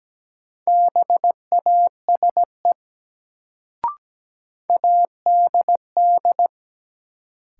Morse code, heard as a keyed tone, about 17 wpm, 700 Hz. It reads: BASE ADD
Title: Sentences from Top 1000 Words - Review Part 1 - New Word Drill - 17wpm